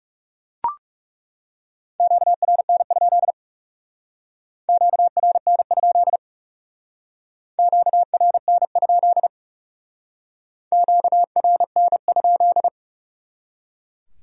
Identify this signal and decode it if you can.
Morse code, heard as a keyed tone, about 36 wpm, 700 Hz. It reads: QRN? QRN? QRN? QRN?
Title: Speed-Racing Q-Codes - 30wpm